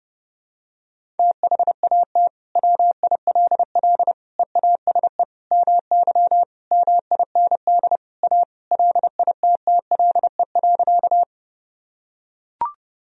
Morse code, heard as a keyed tone, about 30 wpm, 700 Hz. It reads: THAT WILL EASE MY MIND A LITTLE.
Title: Sentences from Top 1000 Words - Review Part 1 - 30wpm